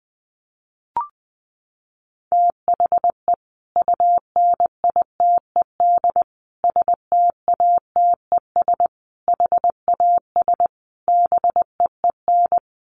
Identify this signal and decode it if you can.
Morse code, heard as a keyed tone, about 19 wpm, 700 Hz. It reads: THE UNITED STATES HAS BEEN
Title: Sets of 5 Words Encore2 20wpm